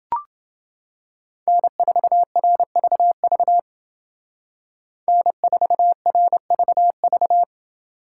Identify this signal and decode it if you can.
Morse code, heard as a keyed tone, about 29 wpm, 700 Hz. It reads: N4RVV N4RVV